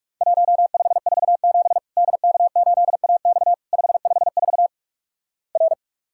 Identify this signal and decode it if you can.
Morse code, heard as a keyed tone, about 47 wpm, 700 Hz. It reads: 1547 DK8AX 554 R
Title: Worked All Europe DX Contest - QTC Exchange - 45wpm